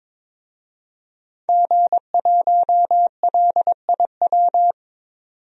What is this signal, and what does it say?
Morse code, heard as a keyed tone, about 22 wpm, 700 Hz. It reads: G1LIW
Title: DXCC-100 Call Signs - 5 Characters - 22wpm